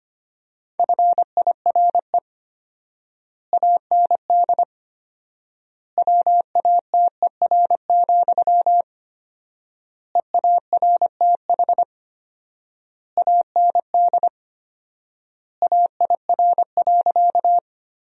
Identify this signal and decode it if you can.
Morse code, heard as a keyed tone, about 25 wpm, 700 Hz. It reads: FIRE AND WATER, EARTH AND AIR.
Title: 4x Word-Spacing - Sentences from Top 400 Words - 25wpm